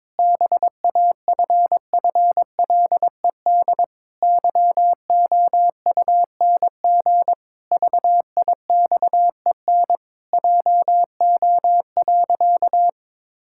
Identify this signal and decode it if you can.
Morse code, heard as a keyed tone, about 22 wpm, 700 Hz. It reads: BAFFLED YOUNG VIXEN JO.
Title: Pangram Challenge v2 - 22wpm